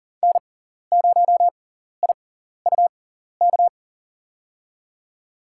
Morse code, heard as a keyed tone, about 40 wpm, spaced as for 14 wpm, 700 Hz. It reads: N0IUK
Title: USA Call Signs - 6x Character Spacing - 40wpm